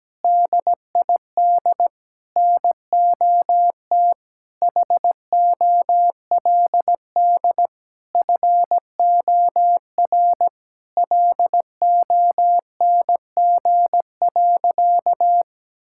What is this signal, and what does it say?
Morse code, heard as a keyed tone, about 17 wpm, 700 Hz. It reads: DID NOT HOLD FOR LONG.
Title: Sentences from Top 1000 Words - Review Part 5 - 17wpm